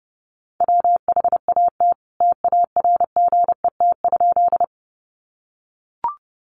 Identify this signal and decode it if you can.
Morse code, heard as a keyed tone, about 30 wpm, 700 Hz. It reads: WHAT TARGET?
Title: Sentences from Top 1200 Words 30wpm